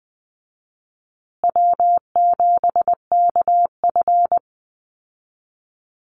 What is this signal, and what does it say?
Morse code, heard as a keyed tone, about 20 wpm, 700 Hz. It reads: W7KF
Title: Easy Call Signs Repeated 3x - 20wpm